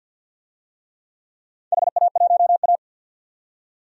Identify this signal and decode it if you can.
Morse code, heard as a keyed tone, about 52 wpm, 700 Hz. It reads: HA1A
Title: DXCC-100 Call Signs - 4 Characters - 50wpm